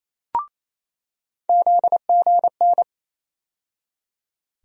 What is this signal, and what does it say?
Morse code, heard as a keyed tone, about 27 wpm, 700 Hz. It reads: ZGN
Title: USA Call Sign Postfixes - 3 Letters 28wpm